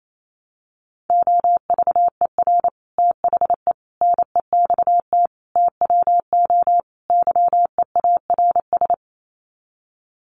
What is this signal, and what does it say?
Morse code, heard as a keyed tone, about 28 wpm, 700 Hz. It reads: OVER THE NEXT TWO YEARS
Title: Sets of 5 Words Encore2 28wpm